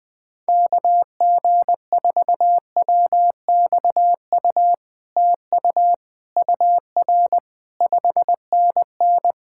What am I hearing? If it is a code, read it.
Morse code, heard as a keyed tone, about 19 wpm, 700 Hz. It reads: KG4WXU TU UR 5NN
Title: POTA QSO Practice - 20wpm